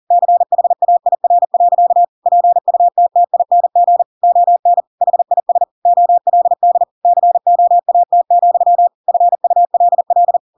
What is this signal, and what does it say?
Morse code, heard as a keyed tone, about 40 wpm, 700 Hz. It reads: CHAIR. PUTTING ON HIS OLD COAT, FULL
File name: Pinocchio / Pinocchio ch8 40 40 ik1hge